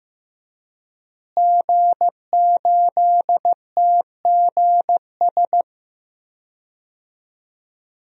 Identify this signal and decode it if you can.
Morse code, heard as a keyed tone, about 15 wpm, 700 Hz. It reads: G8TGS
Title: DXCC-100 Call Signs - 5 Characters - Repeated 2x - 15wpm